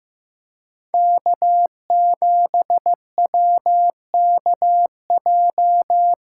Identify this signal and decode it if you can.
Morse code, heard as a keyed tone, about 15 wpm, 700 Hz. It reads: K7WKJ
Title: USA Call Signs - 5 Characters - 6x Character Spacing - 15wpm